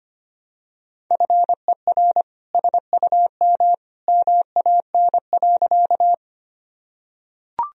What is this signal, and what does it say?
Morse code, heard as a keyed tone, about 25 wpm, 700 Hz. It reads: FER SUM MAN.
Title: Sentences from Top 200 Words and Abbreviations 25wpm